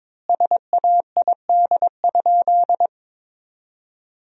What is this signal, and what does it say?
Morse code, heard as a keyed tone, about 21 wpm, 700 Hz. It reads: SAID?
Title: 3x Word-Spacing - Sentences from Top 100 Words - 22wpm